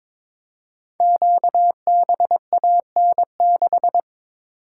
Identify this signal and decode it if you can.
Morse code, heard as a keyed tone, about 22 wpm, 690 Hz. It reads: QBAN6
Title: Code Groups - 5 Letter:Numbers - 22wpm